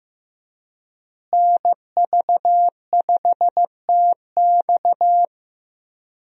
Morse code, heard as a keyed tone, about 15 wpm, 700 Hz. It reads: NV5TX
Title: USA Call Signs - 5 Characters - Repeated 2x - 15wpm